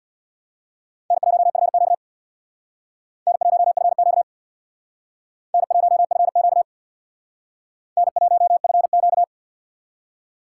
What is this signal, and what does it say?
Morse code, heard as a keyed tone, about 60 wpm, 700 Hz. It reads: N1FQ N1FQ N1FQ N1FQ
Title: Speed Racing - USA 1x2 Call Signs 50wpm